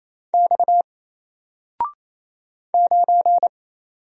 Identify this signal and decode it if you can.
Morse code, heard as a keyed tone, about 27 wpm, 700 Hz. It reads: X 9
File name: Single-Letter-Number-28wpm